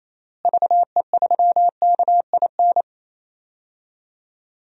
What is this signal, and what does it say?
Morse code, heard as a keyed tone, about 28 wpm, 700 Hz. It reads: VE3KIN
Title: DXCC-100 Call Signs - 6 Characters - Repeated 3x - 28wpm